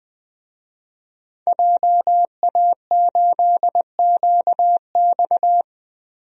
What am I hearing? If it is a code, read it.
Morse code, heard as a keyed tone, about 20 wpm, 700 Hz. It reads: JA8QX
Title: DXCC-100 Call Signs - 5 Characters - Repeated 2x - 20wpm